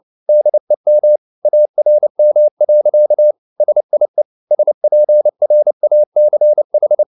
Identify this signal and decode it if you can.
Morse code, heard as a keyed tone, about 29 wpm, 600 Hz. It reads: DEM ARM. SIE SPRACH